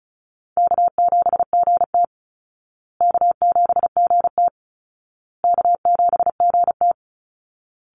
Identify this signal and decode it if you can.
Morse code, heard as a keyed tone, about 34 wpm, 700 Hz. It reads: K7GT K7GT K7GT
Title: Easy Call Signs Repeated 3x - 35wpm